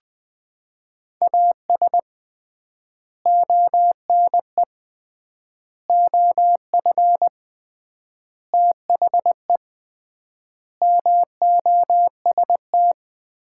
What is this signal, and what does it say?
Morse code, heard as a keyed tone, about 20 wpm, 700 Hz. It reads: AS ONE OF THE MOST
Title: 3x Word-Spacing - Sets of 5 Words - 20wpm